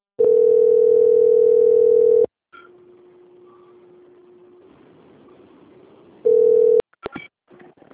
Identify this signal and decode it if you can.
Ringback tone (North America), 440 + 480 Hz.
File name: out- - - - - .15515